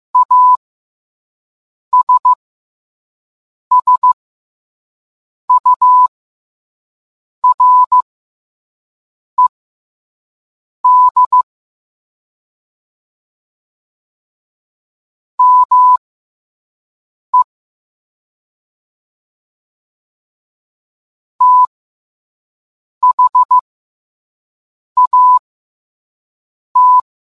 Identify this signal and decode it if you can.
Morse code, heard as a keyed tone, about 15 wpm, spaced as for 5 wpm, 1000 Hz. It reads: ASSURED ME THAT